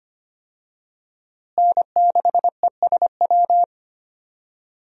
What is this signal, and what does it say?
Morse code, heard as a keyed tone, about 25 wpm, 700 Hz. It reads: N6ESW
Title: USA Call Signs - 5 Characters - Repeated 2x - 25wpm